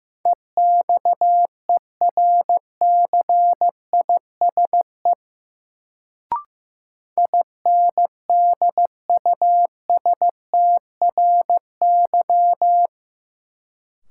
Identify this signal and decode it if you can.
Morse code, heard as a keyed tone, about 15 wpm, 700 Hz. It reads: EXERCISE INDUSTRY
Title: Sentences from Top 1000 Words - Review Part 4 - New Word Drill - 15wpm